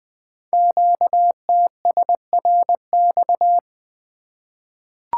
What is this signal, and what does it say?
Morse code, heard as a keyed tone, about 20 wpm, 700 Hz. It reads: QTSRX
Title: Code Groups - 5 Letters - 20wpm